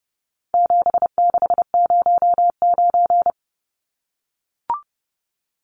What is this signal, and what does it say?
Morse code, heard as a keyed tone, about 30 wpm, 700 Hz. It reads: 7609
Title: Numbers - 4-Digit 30wpm